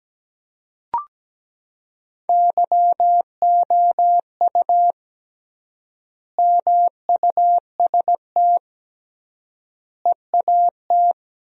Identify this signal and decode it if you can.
Morse code, heard as a keyed tone, about 17 wpm, 700 Hz. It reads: YOU MUST EAT
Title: 3x Word-Spacing - Sentences from Top 300 Words - 17wpm